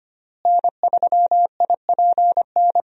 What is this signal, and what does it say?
Morse code, heard as a keyed tone, about 25 wpm, 700 Hz. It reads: N3IPN
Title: USA Call Signs - 5 Characters - Repeated 3x - 25wpm